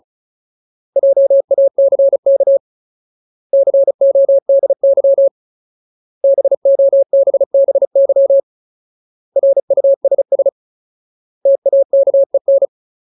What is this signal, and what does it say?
Morse code, heard as a keyed tone, about 34 wpm, 550 Hz. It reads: JACK CODY BOBBY RUSS TAKEN
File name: Ultimate Word List / 35WPM Ultimate